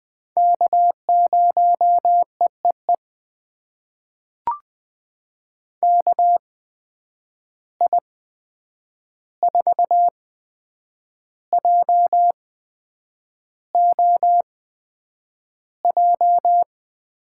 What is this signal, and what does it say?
Morse code, heard as a keyed tone, about 20 wpm, spaced as for 8 wpm, 700 Hz. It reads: K0EEE KI4JOJ
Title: USA Call Signs - 8x Character Spacing - 20wpm